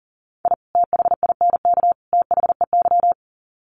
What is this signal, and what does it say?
Morse code, heard as a keyed tone, about 40 wpm, 700 Hz. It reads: I THINK THEY
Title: Sets of 3 Words 40wpm